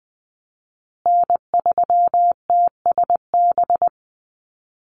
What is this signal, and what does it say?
Morse code, heard as a keyed tone, about 20 wpm, 700 Hz. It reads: N3TSB
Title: USA Call Signs Repeated 2x 20wpm